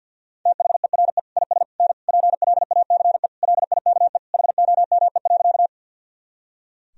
Morse code, heard as a keyed tone, about 50 wpm, 700 Hz. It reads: THERE IS N PLACE LIKE HOME.
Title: Sentences from Top 200 Words and Abbreviations 50wpm